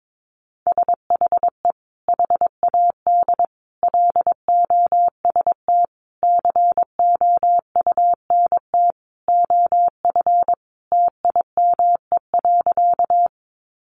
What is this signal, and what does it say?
Morse code, heard as a keyed tone, about 22 wpm, 700 Hz. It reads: SHE HAD LOST COUNT OF TIME.
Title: Sentences from Top 1500 Words 22wpm